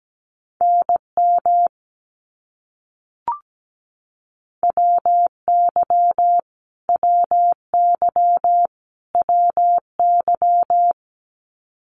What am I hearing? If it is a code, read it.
Morse code, heard as a keyed tone, about 17 wpm, 700 Hz. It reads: NM WY WY WY
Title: US States Repeated 3x 17wpm